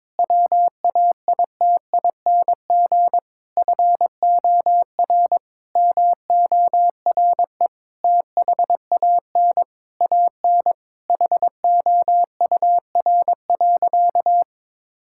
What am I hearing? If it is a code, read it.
Morse code, heard as a keyed tone, about 22 wpm, 700 Hz. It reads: WAITING FOR MORE THAN AN HOUR.
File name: practice / 22wpm / Sentences with 3-Word Phrases - 22wpm